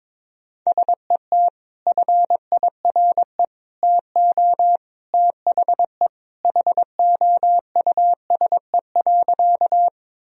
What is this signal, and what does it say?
Morse code, heard as a keyed tone, about 22 wpm, 700 Hz. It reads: SET FIRE TO THE HOUSE.
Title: Sentences from Top 1000 Words - Review Part 1 - 22wpm